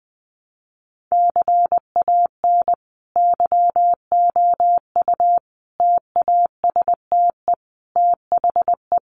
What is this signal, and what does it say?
Morse code, heard as a keyed tone, about 20 wpm, 700 Hz. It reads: CAN YOU TASTE THE
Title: Sentences from Top 1200 Words 20wpm